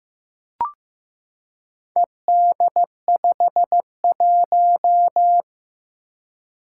Morse code, heard as a keyed tone, about 15 wpm, 700 Hz. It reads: ED51
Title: Four Challenging Letter-Number Combinations 15wpm